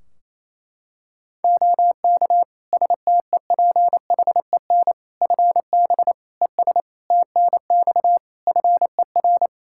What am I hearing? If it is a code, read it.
Morse code, heard as a keyed tone, about 27 wpm, 700 Hz. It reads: OK STEPHEN FB ES TNX FER
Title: Practice QSO - Exchange 2 - Info - 28wpm